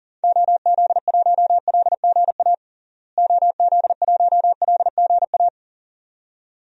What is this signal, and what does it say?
Morse code, heard as a keyed tone, about 37 wpm, 700 Hz. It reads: OZ1LGA OZ1LGA
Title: DXCC-100 Call Signs - 6 Characters - Repeated 2x - 40wpm